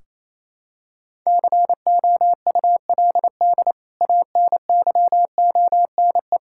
Morse code, heard as a keyed tone, about 28 wpm, 700 Hz. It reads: COULD ANYONE
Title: Sentences from Top 1600 Words 28wpm